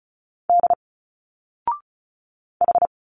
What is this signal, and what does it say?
Morse code, heard as a keyed tone, about 35 wpm, 700 Hz. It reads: D H
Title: Single Letters 35wpm